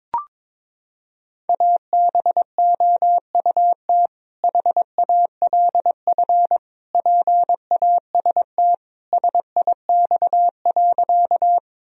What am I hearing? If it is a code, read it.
Morse code, heard as a keyed tone, about 22 wpm, 690 Hz. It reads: ABOUT HALF PAST SIX.
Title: Sentences from Top 1000 Words - Review Part 3 - 22wpm